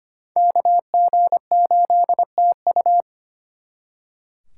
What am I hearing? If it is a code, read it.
Morse code, heard as a keyed tone, about 34 wpm, 700 Hz. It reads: KG8TU